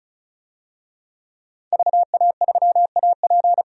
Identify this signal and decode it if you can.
Morse code, heard as a keyed tone, about 34 wpm, 700 Hz. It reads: VA3AP
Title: DXCC-100 Call Signs - 5 Characters - 35wpm